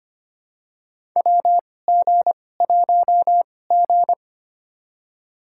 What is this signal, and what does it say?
Morse code, heard as a keyed tone, about 25 wpm, spaced as for 18 wpm, 700 Hz. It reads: WG1G
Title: USA Call Signs - 2x Character Spacing - 25wpm